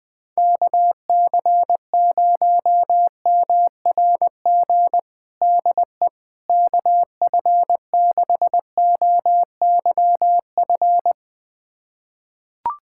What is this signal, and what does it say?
Morse code, heard as a keyed tone, about 20 wpm, 700 Hz. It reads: KC0MRG DE KF6OYF
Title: USA Callsign DE Callsign - 6 Characters - 20wpm